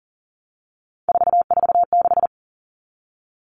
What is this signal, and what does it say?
Morse code, heard as a keyed tone, about 40 wpm, 700 Hz. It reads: 446
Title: Numbers - 3-Digit 40wpm